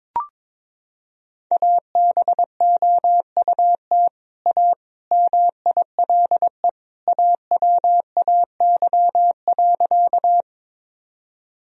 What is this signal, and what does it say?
Morse code, heard as a keyed tone, about 22 wpm, 700 Hz. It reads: ABOUT A MILE AWAY.
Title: Sentences from Top 1000 Words - Review Part 1 - 22wpm